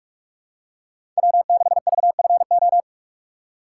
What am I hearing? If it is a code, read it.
Morse code, heard as a keyed tone, about 48 wpm, 700 Hz. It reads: W6VFO
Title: USA Call Signs - 5 Characters - 6x Character Spacing - 45wpm